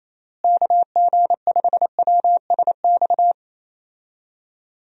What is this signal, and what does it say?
Morse code, heard as a keyed tone, about 28 wpm, 700 Hz. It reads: KG5WSX
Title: USA Call Signs - 6 Characters - Repeated 3x - 28wpm